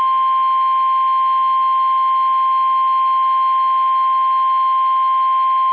Emergency Alert System (SAME) tones (the 1050 Hz attention tone). It is heard from 0.8 to 5.1 s.
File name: eas / KUSO / KUSO